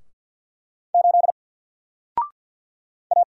Morse code, heard as a keyed tone, about 50 wpm, 700 Hz. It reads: 8 A
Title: Single Character (Letter:Number:Punctuation:Prosign) 50wpm